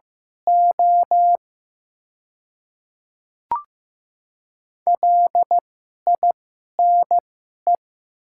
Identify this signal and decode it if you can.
Morse code, heard as a keyed tone, about 15 wpm, spaced as for 11 wpm, 700 Hz. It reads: O LINE
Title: 2x Character Spacing - Top 500 Words - 15wpm